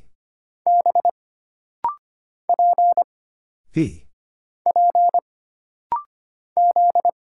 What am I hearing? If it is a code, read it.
Morse code, heard as a keyed tone, about 25 wpm, 700 Hz. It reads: B P P Z
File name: Single Character (Letter:Number:Punctuation:Prosign) - Rapid-Fire - 25wpm